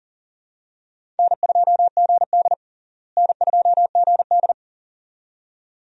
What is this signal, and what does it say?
Morse code, heard as a keyed tone, about 40 wpm, 700 Hz. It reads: N2GD N2GD
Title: USA Call Signs - 4 Characters - Repeated 2x - 40wpm